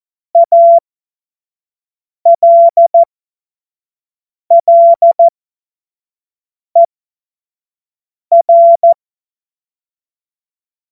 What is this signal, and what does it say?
Morse code, heard as a keyed tone, about 14 wpm, spaced as for 5 wpm, 680 Hz. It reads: ALLER
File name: Morsekurs / Witze / Witz 26. April 2022 25 F70